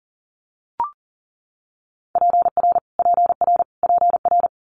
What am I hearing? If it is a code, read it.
Morse code, heard as a keyed tone, about 38 wpm, 700 Hz. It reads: PR PR PR
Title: Multi-letter Phonemes Repeated 3x 40wpm